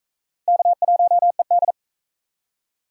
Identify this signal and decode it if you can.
Morse code, heard as a keyed tone, about 38 wpm, 700 Hz. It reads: K1ED